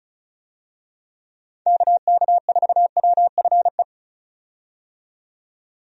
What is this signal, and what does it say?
Morse code, heard as a keyed tone, about 36 wpm, 700 Hz. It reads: KK4WFE
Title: USA Call Signs - 6 Characters - 35wpm